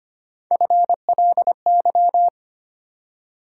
Morse code, heard as a keyed tone, about 25 wpm, 700 Hz. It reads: FLY
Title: Three Letter Words - 25wpm